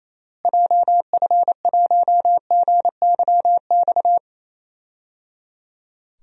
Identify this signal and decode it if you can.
Morse code, heard as a keyed tone, about 28 wpm, 700 Hz. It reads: JF1GYX
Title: DXCC-100 Call Signs - 6 Characters - Repeated 3x - 28wpm